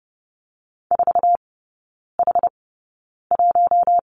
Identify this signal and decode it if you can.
Morse code, heard as a keyed tone, about 30 wpm, spaced as for 9 wpm, 700 Hz. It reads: 4H1
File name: Lesson 15e - Number 5 - Three Characters 30wpm